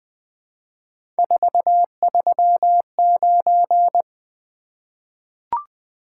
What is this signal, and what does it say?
Morse code, heard as a keyed tone, about 20 wpm, 700 Hz. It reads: 439
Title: Three Letter-Number Combinations - 20wpm